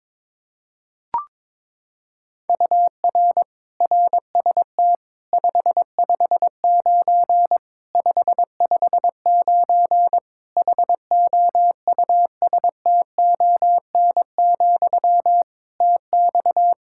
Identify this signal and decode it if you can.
Morse code, heard as a keyed tone, about 22 wpm, 700 Hz. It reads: UR RST 559 559 HOUSTON, TX
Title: K1USN SST QSO Practice 22wpm